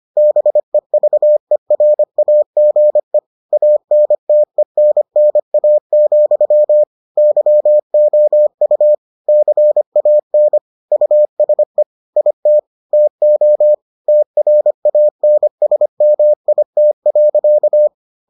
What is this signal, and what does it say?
Morse code, heard as a keyed tone, about 25 wpm, 600 Hz. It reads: BEVERAGE ANTENNA, YOU CAN USE IT TO TRANSMIT.